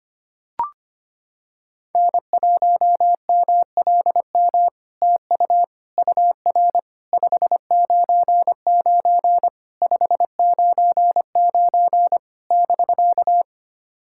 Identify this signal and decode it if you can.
Morse code, heard as a keyed tone, about 25 wpm, 700 Hz. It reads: N1MLM TU UR 599 599 <BK>
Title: SOTA QSO Practice 25wpm